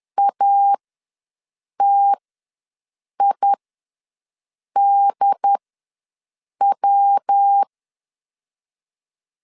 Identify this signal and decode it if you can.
Morse code, heard as a keyed tone, about 10 wpm, spaced as for 6 wpm, 800 Hz. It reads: ATIDW
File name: LEZIONI / 5 LBVC / 5 LBV ws 10wpm 3645